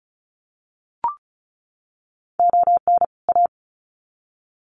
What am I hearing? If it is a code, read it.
Morse code, heard as a keyed tone, about 34 wpm, 700 Hz. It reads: ON A
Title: Sets of 2 Words 35wpm